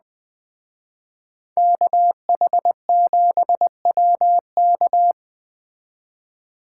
Morse code, heard as a keyed tone, about 20 wpm, 700 Hz. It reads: KH7WK